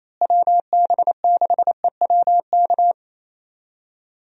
Morse code, heard as a keyed tone, about 28 wpm, 700 Hz. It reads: WB6EWK
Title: USA Call Signs - 6 Characters - Repeated 3x - 28wpm